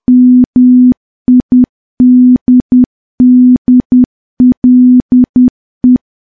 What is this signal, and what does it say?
Morse code, heard as a keyed tone, about 10 wpm, 250 Hz. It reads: MIDDLE
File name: morse / morse-training-10WPM